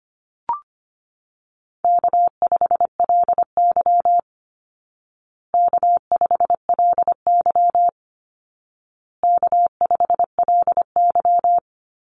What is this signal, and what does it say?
Morse code, heard as a keyed tone, about 25 wpm, 700 Hz. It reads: K5LY K5LY K5LY
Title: Call Signs Repeated 2x - 25wpm